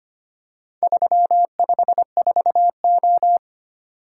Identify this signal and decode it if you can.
Morse code, heard as a keyed tone, about 25 wpm, 700 Hz. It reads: 354O
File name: practice / 25wpm / Four Letter-Number Combinations - 25wpm